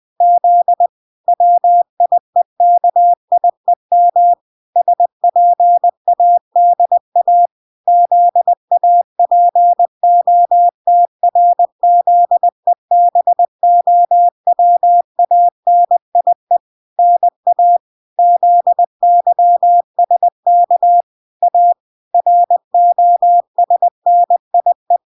Morse code, heard as a keyed tone, about 20 wpm, 700 Hz. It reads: Z WIEKIEM SPADA ZAPOTRZEBOWANIE NA ZYSK A ROSNIE